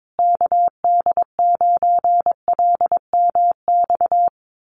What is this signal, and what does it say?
Morse code, heard as a keyed tone, about 22 wpm, 700 Hz. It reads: KD9LMX
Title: USA Call Signs Repeated 3x 22wpm